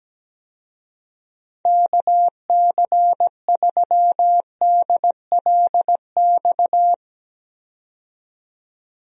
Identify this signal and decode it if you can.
Morse code, heard as a keyed tone, about 17 wpm, 700 Hz. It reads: KC3DLX
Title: USA Call Signs - 6 Characters - 17wpm